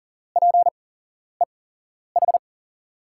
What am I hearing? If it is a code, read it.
Morse code, heard as a keyed tone, about 40 wpm, spaced as for 11 wpm, 700 Hz. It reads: PEH
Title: USA Call Signs - 8x Character Spacing - 40wpm